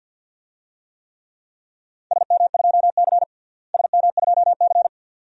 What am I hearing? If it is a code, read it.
Morse code, heard as a keyed tone, about 50 wpm, 700 Hz. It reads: SM2C SM2C
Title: DXCC-100 Call Signs - 4 Characters - Repeated 2x - 50wpm